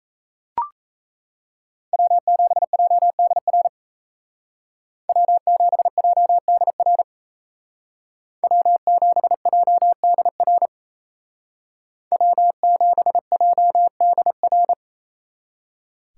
Morse code, heard as a keyed tone, about 35 wpm, 700 Hz. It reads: W7JDR W7JDR W7JDR W7JDR